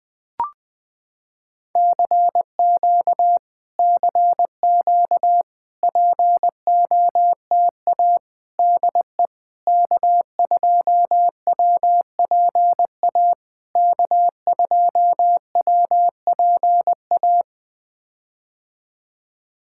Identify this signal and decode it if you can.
Morse code, heard as a keyed tone, about 20 wpm, 700 Hz. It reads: CQ CQ POTA DE K2WPA K2WPA
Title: POTA QSO Practice - 20wpm